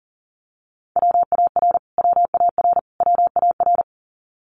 Morse code, heard as a keyed tone, about 40 wpm, 700 Hz. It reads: WAR WAR WAR
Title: Top 500 Words Repeated 40wpm